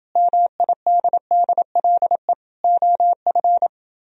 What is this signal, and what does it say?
Morse code, heard as a keyed tone, about 28 wpm, 700 Hz. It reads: MIDDLE OF